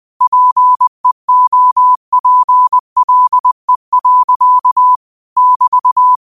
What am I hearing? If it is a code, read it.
Morse code, heard as a keyed tone, about 20 wpm, 1000 Hz. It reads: PEOPLE. <BT>
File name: qotdfiles / 20wpm